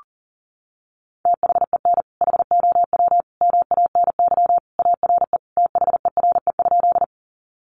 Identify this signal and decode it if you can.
Morse code, heard as a keyed tone, about 40 wpm, 700 Hz. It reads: THEN HOW MANY ARE THERE?
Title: Sentences from Top 100 Words 40wpm